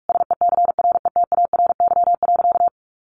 Morse code, heard as a keyed tone, about 46 wpm, 700 Hz. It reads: SECRETARY.
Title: Sentences from Top 1300 Words 45wpm